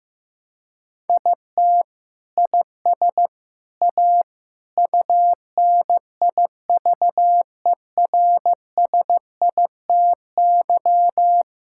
Morse code, heard as a keyed tone, about 15 wpm, 700 Hz. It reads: IT IS A UNIVERSITY
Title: Sentences from Top 1000 Words - Review Part 5 - 15wpm